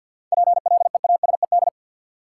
Morse code, heard as a keyed tone, about 52 wpm, 700 Hz. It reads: PLEASED
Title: Seven Letter Words - 50wpm